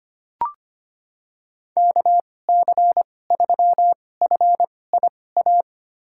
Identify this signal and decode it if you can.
Morse code, heard as a keyed tone, about 25 wpm, spaced as for 18 wpm, 700 Hz. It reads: KC3FIA
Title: USA Call Signs - 2x Character Spacing - 25wpm